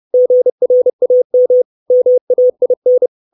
Morse code, heard as a keyed tone, about 30 wpm, 500 Hz. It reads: GRAM MAIN
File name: 632 Words 30WPM